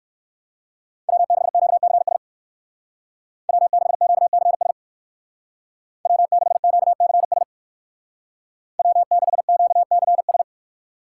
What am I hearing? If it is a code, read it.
Morse code, heard as a keyed tone, about 54 wpm, 700 Hz. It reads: W6QCS W6QCS W6QCS W6QCS
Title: Speed Racing - USA 1x3 Call Signs 45wpm